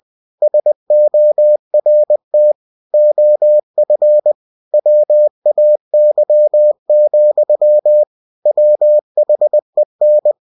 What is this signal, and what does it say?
Morse code, heard as a keyed tone, about 20 wpm, 600 Hz. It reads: SORT OF WAY, WHEN
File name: cw-books / winnie-the-pooh / winnie-20wpm-page-0039